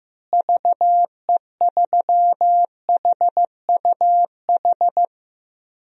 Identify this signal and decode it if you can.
Morse code, heard as a keyed tone, about 15 wpm, 700 Hz. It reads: VE3HUH
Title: DXCC-100 Call Signs - 6 Characters - Repeated 3x - 15wpm